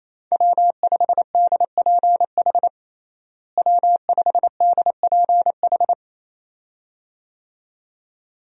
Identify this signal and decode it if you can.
Morse code, heard as a keyed tone, about 28 wpm, 700 Hz. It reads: W5DPH W5DPH
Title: USA Call Signs - 5 Characters - Repeated 2x - 28wpm